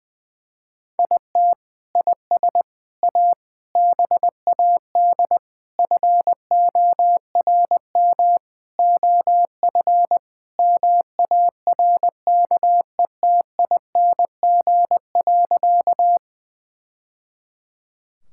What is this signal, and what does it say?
Morse code, heard as a keyed tone, about 20 wpm, 700 Hz. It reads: IT IS A BAD FORM OF MARKETING.
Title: Sentences from Top 1000 Words - Review Part 5 - 20wpm